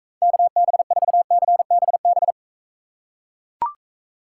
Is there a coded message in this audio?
Yes: Morse code from 0.2 to 2.3 s, heard as a keyed tone, about 45 wpm, 700 Hz. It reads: KB4CBB